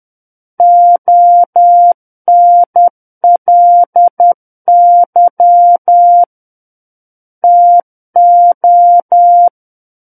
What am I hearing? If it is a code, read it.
Morse code, heard as a keyed tone, about 10 wpm, 700 Hz. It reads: ONLY TO
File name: Book One / 01-Chapter 1 - 10 WPM